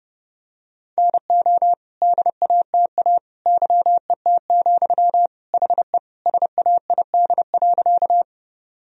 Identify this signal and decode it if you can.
Morse code, heard as a keyed tone, about 30 wpm, 700 Hz. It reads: NO DATA YET, HE SAID.
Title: Sentences from Top 1000 Words - Review Part 3 - 30wpm